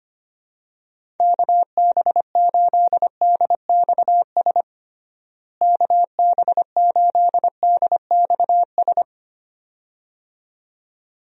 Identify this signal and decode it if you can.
Morse code, heard as a keyed tone, about 25 wpm, 700 Hz. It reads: KB8DXS KB8DXS
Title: USA Call Signs - 6 Characters - Repeated 2x - 25wpm